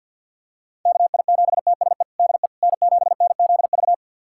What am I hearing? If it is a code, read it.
Morse code, heard as a keyed tone, about 47 wpm, 700 Hz. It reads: KI7TSE DE N7NZV
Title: USA Callsign DE Callsign 50wpm